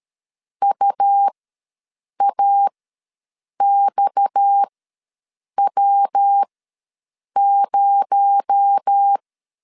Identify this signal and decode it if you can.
Morse code, heard as a keyed tone, about 13 wpm, spaced as for 7 wpm, 800 Hz. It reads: UAXW0
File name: LEZIONI / 6 PKX0 / 6 PKX0 ws 12wpm 3448